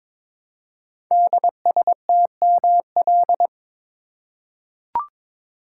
Morse code, heard as a keyed tone, about 22 wpm, 700 Hz. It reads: DSTML
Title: Code Groups - 5 Letters - 22wpm